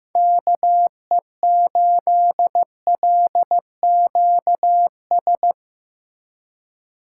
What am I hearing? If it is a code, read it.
Morse code, heard as a keyed tone, about 17 wpm, 700 Hz. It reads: KE8LQS
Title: Speed Racing - USA 2x3 Call Signs 15wpm